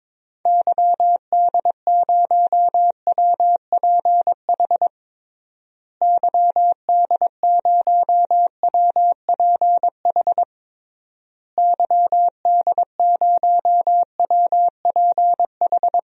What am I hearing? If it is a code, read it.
Morse code, heard as a keyed tone, about 22 wpm, 700 Hz. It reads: YD0WPH YD0WPH YD0WPH
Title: DXCC-100 Call Signs - 6 Characters - Repeated 3x - 22wpm